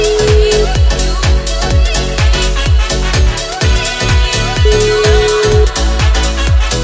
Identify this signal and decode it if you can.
Ringback tone (Europe, 425 Hz), with noise.